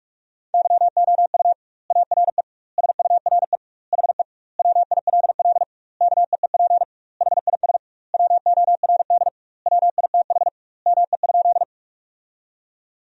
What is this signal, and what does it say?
Morse code, heard as a keyed tone, about 41 wpm, 700 Hz. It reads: YOU ARE SURE HE WILL KEEP HIS WORD WITH ME?
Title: 2x Word-Spacing - Sentences from Top 300 Words - 45wpm